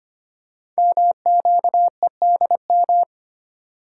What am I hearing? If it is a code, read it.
Morse code, heard as a keyed tone, about 25 wpm, 700 Hz. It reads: MQEDM
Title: Code Groups - 5 Letter:Numbers - 25wpm